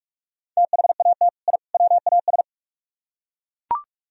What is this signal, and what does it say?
Morse code, heard as a keyed tone, about 41 wpm, 700 Hz. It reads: THAT I WAS